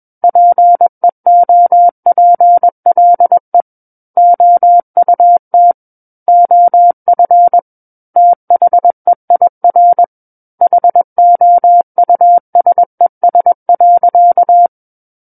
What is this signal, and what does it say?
Morse code, heard as a keyed tone, about 21 wpm, 700 Hz. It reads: PEOPLE OUT OF THEIR HOUSES.
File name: Book One / 12-Chapter 12 - 21 WPM